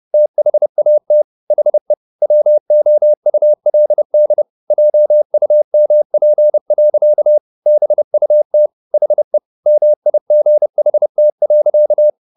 Morse code, heard as a keyed tone, about 30 wpm, 600 Hz. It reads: THAT HE WOULD JUMP. BUT HE MIGHT.
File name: the old man and the sea / the old man and the sea 1 30wpm